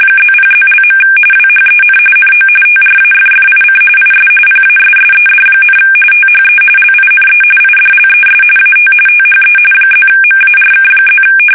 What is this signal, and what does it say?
RTTY, about 45.45 baud. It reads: +0+). WHEN NO TRAFFIC IS PASSED, THE LINE IDLES AT THE +MARK+
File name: fsk850 / teletype-RTTY